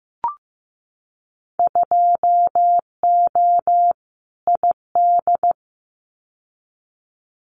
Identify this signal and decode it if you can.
Morse code, heard as a keyed tone, about 15 wpm, 700 Hz. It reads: 2O ID
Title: Winter Field Day Exchanges 15wpm